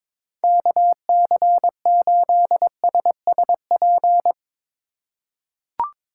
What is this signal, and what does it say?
Morse code, heard as a keyed tone, about 22 wpm, 700 Hz. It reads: KC8SSP
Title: USA Call Signs - 8x Character Spacing - 22wpm